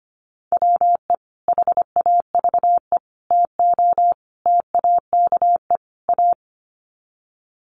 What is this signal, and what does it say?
Morse code, heard as a keyed tone, about 25 wpm, 700 Hz. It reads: WE HAVE TO TAKE A
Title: Sets of 5 Words Encore 25wpm